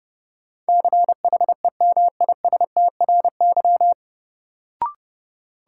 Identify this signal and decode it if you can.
Morse code, heard as a keyed tone, about 30 wpm, 700 Hz. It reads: CHEMISTRY
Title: Sentences from Top 1000 Words - Review Part 6 - New Word Drill - 30wpm